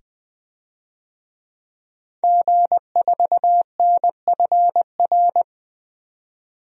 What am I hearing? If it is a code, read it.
Morse code, heard as a keyed tone, about 20 wpm, 700 Hz. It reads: G4NFR
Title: DXCC-100 Call Signs - 5 Characters - 20wpm